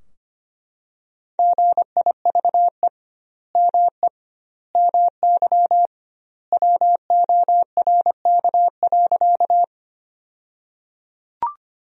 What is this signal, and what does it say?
Morse code, heard as a keyed tone, about 25 wpm, 700 Hz. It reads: GIVE ME MY WORK.
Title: 2x Word-Spacing - Sentences from Top 200 Words - 25wpm